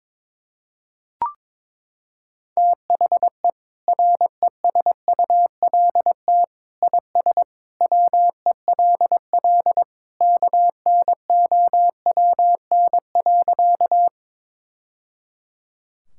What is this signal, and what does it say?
Morse code, heard as a keyed tone, about 22 wpm, 700 Hz. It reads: THE RESULT IS WELL KNOWN.
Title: Sentences from Top 1000 Words - Review Part 5 - 22wpm